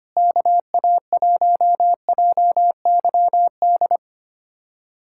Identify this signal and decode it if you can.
Morse code, heard as a keyed tone, about 25 wpm, 700 Hz. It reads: KA1JYD
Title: USA Callsign DE Callsign 25wpm